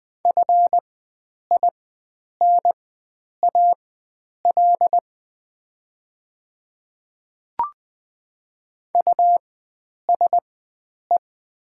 Morse code, heard as a keyed tone, about 20 wpm, spaced as for 9 wpm, 700 Hz. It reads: FINAL USE
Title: 4x Character Spacing - Top 500 Words - 20wpm